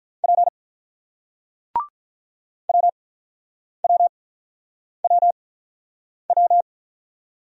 Morse code, heard as a keyed tone, about 46 wpm, spaced as for 9 wpm, 700 Hz. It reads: P WWWW